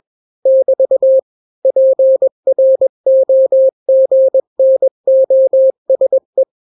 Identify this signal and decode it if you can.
Morse code, heard as a keyed tone, about 21 wpm, 530 Hz. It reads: <BT> PROGNOSE
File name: daily / 20wpm